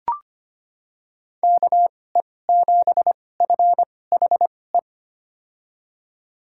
Morse code, heard as a keyed tone, about 25 wpm, spaced as for 18 wpm, 700 Hz. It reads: KE7FHE
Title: USA Call Signs - 6 Characters - 2x Character Spacing - 25wpm